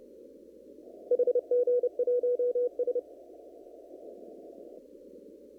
Morse code, heard as a keyed tone, about 29 wpm, 490 Hz. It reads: HG1S
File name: CW / kt5j wpx cw-op-k5pi